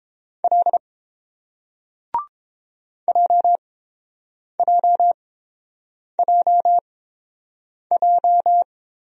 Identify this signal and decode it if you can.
Morse code, heard as a keyed tone, about 30 wpm, spaced as for 7 wpm, 700 Hz. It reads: L JJJJ